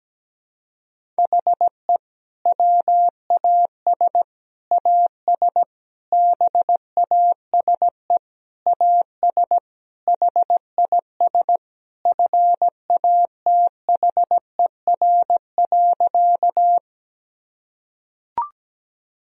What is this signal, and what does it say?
Morse code, heard as a keyed tone, about 17 wpm, 700 Hz. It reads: HE WAS AS BASE AS HIS FATHER.
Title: Sentences from Top 1000 Words - Review Part 1 - 17wpm